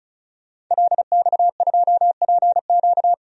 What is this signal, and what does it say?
Morse code, heard as a keyed tone, about 34 wpm, 700 Hz. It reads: LX2PQ
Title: DXCC-100 Call Signs - 5 Characters - 35wpm